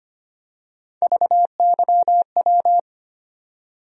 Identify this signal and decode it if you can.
Morse code, heard as a keyed tone, about 25 wpm, 700 Hz. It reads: VYW
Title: USA Call Sign Postfixes - 3 Letters 25wpm